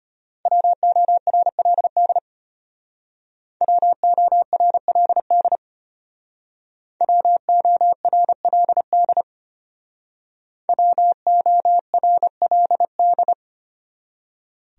Morse code, heard as a keyed tone, about 33 wpm, 700 Hz. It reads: WORLD WORLD WORLD WORLD